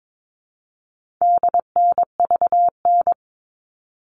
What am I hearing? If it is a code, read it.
Morse code, heard as a keyed tone, about 22 wpm, 700 Hz. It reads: DNVN
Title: All Cut Numbers - 4-Digit 22wpm